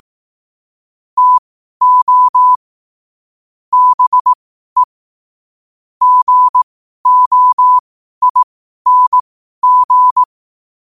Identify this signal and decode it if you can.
Morse code, heard as a keyed tone, about 18 wpm, spaced as for 12 wpm, 1000 Hz. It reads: TO BE GOING